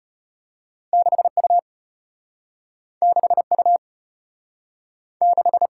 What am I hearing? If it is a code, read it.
Morse code, heard as a keyed tone, about 33 wpm, 700 Hz. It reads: 6U 6U 6